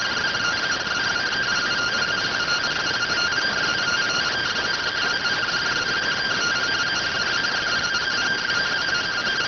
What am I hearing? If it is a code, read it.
Two-tone teleprinter signal, about 170 Hz shift.